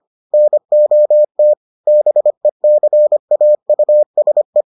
Morse code, heard as a keyed tone, about 25 wpm, 600 Hz. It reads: NOT BECAUSE